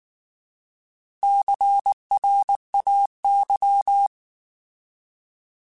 Morse code, heard as a keyed tone, about 19 wpm, 800 Hz. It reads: CRAY